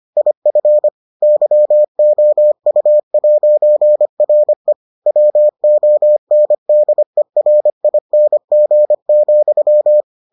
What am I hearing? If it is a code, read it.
Morse code, heard as a keyed tone, about 25 wpm, 600 Hz. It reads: IF YOU'RE WONDERING,